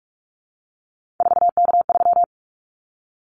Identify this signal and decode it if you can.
Morse code, heard as a keyed tone, about 44 wpm, 700 Hz. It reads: 4K3
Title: Callsign Prefixes for Top 100 DXCC - 45wpm